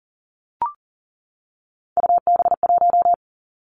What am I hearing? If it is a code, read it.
Morse code, heard as a keyed tone, about 38 wpm, 690 Hz. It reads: UB1
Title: Three Challenging Letter-Number Combinations 40wpm